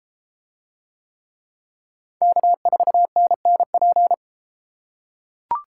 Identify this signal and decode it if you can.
Morse code, heard as a keyed tone, about 32 wpm, 700 Hz. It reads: K4NNP